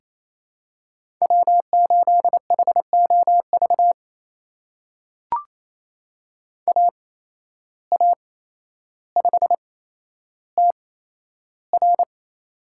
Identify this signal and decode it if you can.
Morse code, heard as a keyed tone, about 28 wpm, spaced as for 12 wpm, 700 Hz. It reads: W8HOV AA5TR
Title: USA Call Signs - 5 Characters - 8x Character Spacing - 28wpm